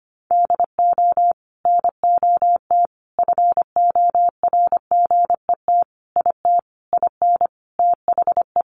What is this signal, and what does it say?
Morse code, heard as a keyed tone, about 25 wpm, 700 Hz. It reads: DO NOT FORGET IT IN THE
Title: Sentences from Top 1200 Words 25wpm